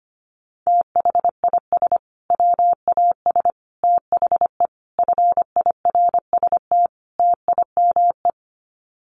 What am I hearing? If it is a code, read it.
Morse code, heard as a keyed tone, about 25 wpm, 700 Hz. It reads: THIS WAS THE FIRST TIME
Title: Sets of 5 Words Encore 25wpm